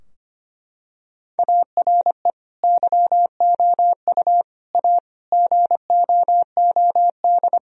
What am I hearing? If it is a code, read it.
Morse code, heard as a keyed tone, about 25 wpm, 700 Hz. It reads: ARE YOU A GOOD
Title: Sentences from Top 1000 Words - Review Part 4 - 25wpm